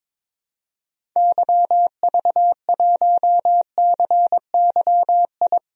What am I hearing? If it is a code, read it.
Morse code, heard as a keyed tone, about 22 wpm, 700 Hz. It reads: YV1CYI
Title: DXCC-100 Call Signs - 6 Characters - Repeated 2x - 22wpm